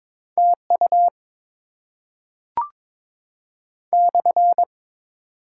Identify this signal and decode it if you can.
Morse code, heard as a keyed tone, about 22 wpm, 700 Hz. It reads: TU /
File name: Top 90 QSO Elements - 6x Character Spacing - 22wpm